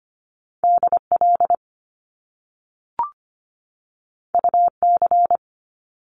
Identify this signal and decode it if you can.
Morse code, heard as a keyed tone, about 25 wpm, 700 Hz. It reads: DL UC
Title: Top Two Letter Combinations in Words 25wpm